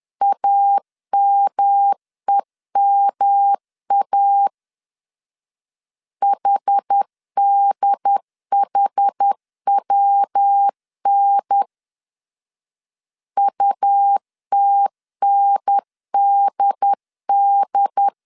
Morse code, heard as a keyed tone, about 10 wpm, 800 Hz. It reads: AMEMA HDHWN UTNDD
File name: CW / DET / 1 3 1500C S B 10wpm 3335